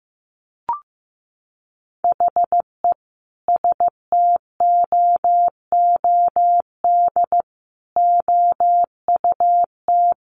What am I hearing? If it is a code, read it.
Morse code, heard as a keyed tone, about 15 wpm, 700 Hz. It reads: HE STOOD OUT
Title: Sentences from Top 1500 Words 15wpm